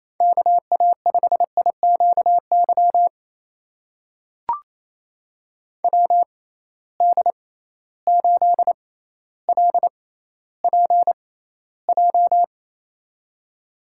Morse code, heard as a keyed tone, about 28 wpm, spaced as for 14 wpm, 700 Hz. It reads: KA5IQY WD8LPJ
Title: USA Call Signs - 6 Characters - 6x Character Spacing - 28wpm